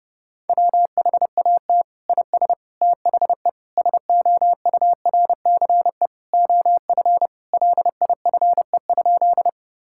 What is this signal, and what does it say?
Morse code, heard as a keyed tone, about 32 wpm, 700 Hz. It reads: WHAT IS THE SOURCE OF LIFE?
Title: Sentences from Top 1000 Words - Review Part 5 - 30wpm